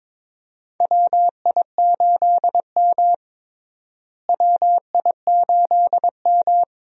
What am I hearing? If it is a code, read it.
Morse code, heard as a keyed tone, about 21 wpm, 700 Hz. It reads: WI8M WI8M
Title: USA Call Signs - 4 Characters - Repeated 2x - 22wpm